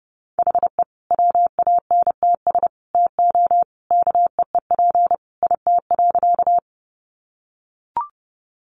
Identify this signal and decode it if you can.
Morse code, heard as a keyed tone, about 30 wpm, 700 Hz. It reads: HE WANTS TO KEEP IT.
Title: Additional Easy Phrases 30wpm